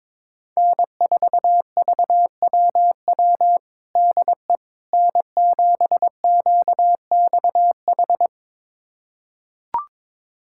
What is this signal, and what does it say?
Morse code, heard as a keyed tone, about 22 wpm, 700 Hz. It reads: N4VWW DE N7QXH
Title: USA Callsign DE Callsign - 5 Characters - 22wpm